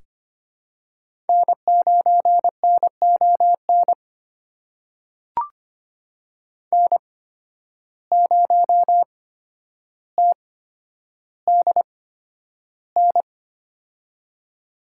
Morse code, heard as a keyed tone, about 25 wpm, spaced as for 11 wpm, 700 Hz. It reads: N9NON N0TDN
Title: USA Call Signs - 5 Characters - 8x Character Spacing - 25wpm